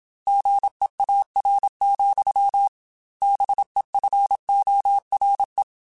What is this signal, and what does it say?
Morse code, heard as a keyed tone, about 27 wpm, 800 Hz. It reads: GEAR, BEFORE
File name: Chuckles / Chuckles-mp3s / BrainMouth @25